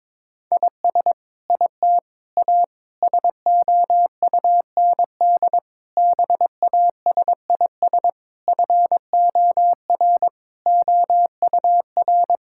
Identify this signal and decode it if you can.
Morse code, heard as a keyed tone, about 21 wpm, 700 Hz. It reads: IS IT A SOUND BASIS FOR OUR
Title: Sentences from Top 1000 Words - Review Part 5 - 22wpm